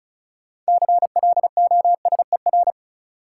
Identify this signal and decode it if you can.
Morse code, heard as a keyed tone, about 35 wpm, 700 Hz. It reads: CLOSER
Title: Sets of 5 Words Encore - New Word Drill - 35wpm